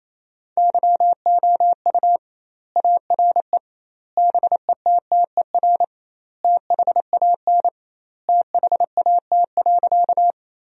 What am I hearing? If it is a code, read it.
Morse code, heard as a keyed tone, about 28 wpm, 700 Hz. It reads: YOU ARE BETTER THAN THAT.
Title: 2x Word-Spacing - Sentences from Top 400 Words - 28wpm